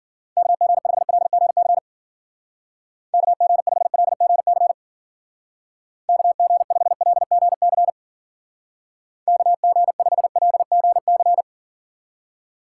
Morse code, heard as a keyed tone, about 50 wpm, 700 Hz. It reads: KG5LGC KG5LGC KG5LGC KG5LGC
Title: Speed Racing - USA 2x3 Call Signs 40wpm